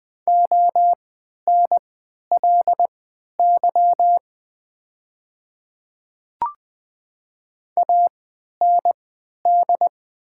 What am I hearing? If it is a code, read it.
Morse code, heard as a keyed tone, about 20 wpm, spaced as for 11 wpm, 700 Hz. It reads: ONLY AND
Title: 3x Character Spacing - Top 200 Words - 20wpm